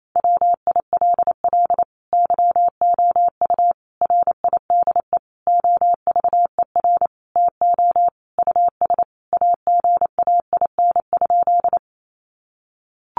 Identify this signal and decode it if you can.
Morse code, heard as a keyed tone, about 28 wpm, 700 Hz. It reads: WILL YOU RIDE OVER TO US AGAIN?
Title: Sentences from Top 1200 Words 28wpm